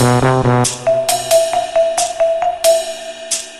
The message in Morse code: none